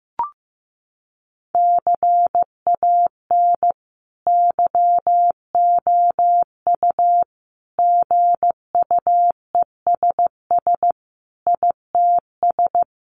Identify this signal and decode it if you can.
Morse code, heard as a keyed tone, about 15 wpm, 700 Hz. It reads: CAN YOU GUESS ITS
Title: Sentences from Top 1500 Words 15wpm